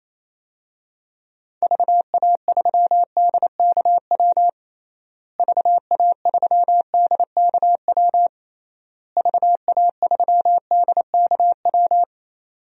Morse code, heard as a keyed tone, about 28 wpm, 700 Hz. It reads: VA3DKW VA3DKW VA3DKW
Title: DXCC-100 Call Signs - 6 Characters - Repeated 3x - 28wpm